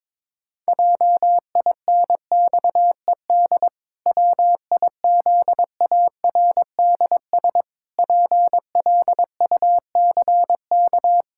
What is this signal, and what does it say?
Morse code, heard as a keyed tone, about 22 wpm, 700 Hz. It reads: JINXED WIZARDS PLUCK
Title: Pangram Challenge v2 - 22wpm